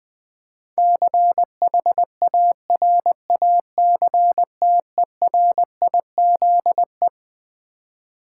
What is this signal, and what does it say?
Morse code, heard as a keyed tone, about 20 wpm, 700 Hz. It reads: CHARACTERIZE
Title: Build A Word - 4 Letters At A Time - 12-Letter Words - 20wpm